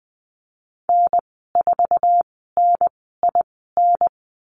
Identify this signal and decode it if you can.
Morse code, heard as a keyed tone, about 20 wpm, spaced as for 14 wpm, 700 Hz. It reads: N4NIN
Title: 069 Lesson 12g - Letter d - Call Signs 20wpm